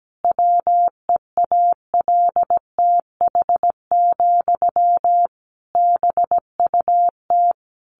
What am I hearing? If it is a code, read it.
Morse code, heard as a keyed tone, about 17 wpm, 700 Hz. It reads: WEALTH, BUT
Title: Sentences from Top 1200 Words 17wpm